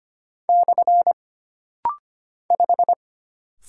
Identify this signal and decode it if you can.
Morse code, heard as a keyed tone, about 25 wpm, 700 Hz. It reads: / 5
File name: Single Character (Letter:Number:Punctuation:Prosign) - Rapid-Fire - 25wpm